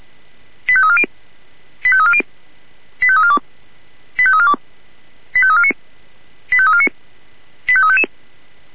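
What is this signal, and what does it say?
Five-tone selective call (ZVEI), heard as quick runs of five tones, tones of about 70 ms. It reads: 85340 85340 85342 85342 85349 85349 85355
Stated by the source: ZVEI five-tone selective call received by SDR.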